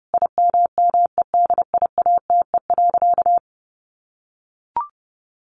Morse code, heard as a keyed tone, about 30 wpm, 700 Hz. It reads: IMMEDIATE.
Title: Sentences from Top 1300 Words 30wpm